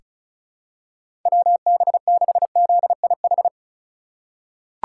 Morse code, heard as a keyed tone, about 34 wpm, 700 Hz. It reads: WB6ZIH
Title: USA Call Signs - 6 Characters - 4x Character Spacing - 35wpm